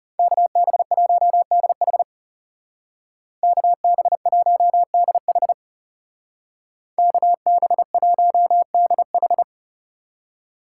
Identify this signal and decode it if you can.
Morse code, heard as a keyed tone, about 39 wpm, 700 Hz. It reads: KB1DH KB1DH KB1DH